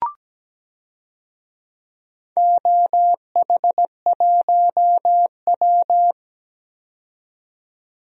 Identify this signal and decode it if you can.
Morse code, heard as a keyed tone, about 17 wpm, 700 Hz. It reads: OH1W
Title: DXCC-100 Call Signs - 4 Characters - 17wpm